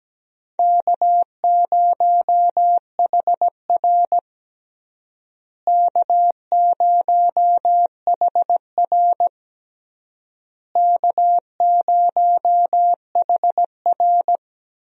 Morse code, heard as a keyed tone, about 17 wpm, 700 Hz. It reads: K0HR K0HR K0HR
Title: USA Call Signs - 4 Characters - Repeated 3x - 17wpm